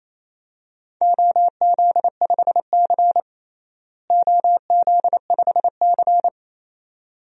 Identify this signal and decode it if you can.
Morse code, heard as a keyed tone, about 28 wpm, 700 Hz. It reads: OZ5C OZ5C
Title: DXCC-100 Call Signs - 4 Characters - Repeated 2x - 28wpm